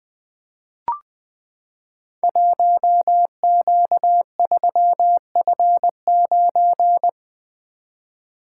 Morse code, heard as a keyed tone, about 20 wpm, 700 Hz. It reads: 1Q3F9
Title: Code Groups - 5 Letter:Numbers - 20wpm